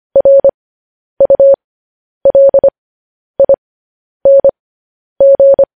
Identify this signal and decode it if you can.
Morse code, heard as a keyed tone, about 25 wpm, spaced as for 10 wpm, 550 Hz. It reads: RULING
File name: podcast / News headlines-1x-25x10